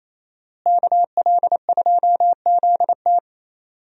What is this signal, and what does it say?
Morse code, heard as a keyed tone, about 28 wpm, 700 Hz. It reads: KL2ZT
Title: USA Call Signs - 5 Characters - Repeated 2x - 28wpm